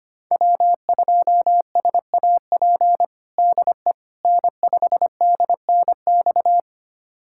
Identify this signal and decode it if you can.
Morse code, heard as a keyed tone, about 25 wpm, 700 Hz. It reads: W2SAP DE N5DNX
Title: USA Callsign DE Callsign - 5 Characters - 25wpm